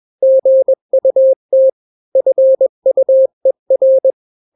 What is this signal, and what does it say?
Morse code, heard as a keyed tone, about 21 wpm, 530 Hz. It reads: GUT FUER
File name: daily / 20wpm